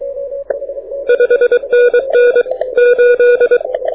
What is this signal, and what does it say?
Morse code, heard as a keyed tone, about 23 wpm, 1460 Hz, with noise. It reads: 5NN 8
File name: audio zawody / IARU 2015 CW SN7Q